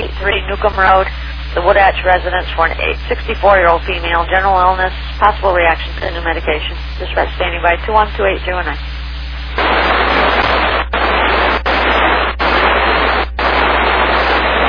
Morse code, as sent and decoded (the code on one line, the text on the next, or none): none